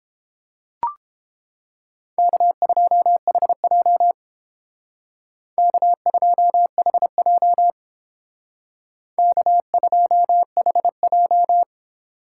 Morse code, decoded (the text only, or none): K2HJ K2HJ K2HJ